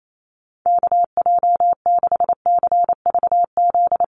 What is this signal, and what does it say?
Morse code, heard as a keyed tone, about 28 wpm, 700 Hz. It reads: KJ6CVZ
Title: USA Call Signs Repeated 2x 28wpm